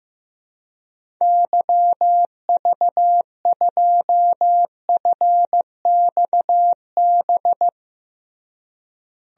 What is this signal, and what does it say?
Morse code, heard as a keyed tone, about 15 wpm, 700 Hz. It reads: YV2FXB
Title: DXCC-100 Call Signs - 6 Characters - Repeated 3x - 15wpm